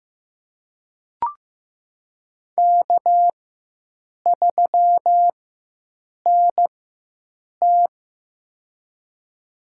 Morse code, heard as a keyed tone, about 15 wpm, spaced as for 7 wpm, 700 Hz. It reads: K3NT
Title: USA Call Signs - 4 Characters - 4x Character Spacing - 15wpm